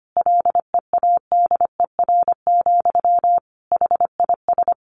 Morse code, heard as a keyed tone, about 25 wpm, 700 Hz. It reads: LEADER, HIS
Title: A Sound Of Thunder 25wpm